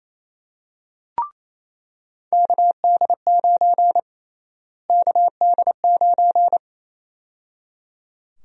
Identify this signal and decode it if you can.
Morse code, heard as a keyed tone, about 27 wpm, 700 Hz. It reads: KD9 KD9
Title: USA Call Sign Prefixes - Repeated 2x - 28wpm